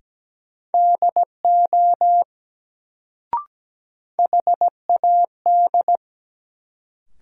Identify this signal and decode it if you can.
Morse code, heard as a keyed tone, about 17 wpm, 700 Hz. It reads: DO HAD
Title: Sentences from Top 100 Words - New Word Drill - 17wpm